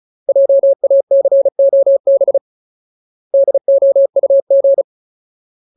Morse code, heard as a keyed tone, about 34 wpm, 550 Hz. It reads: JACOB DOUG
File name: Ultimate Word List / 35WPM Ultimate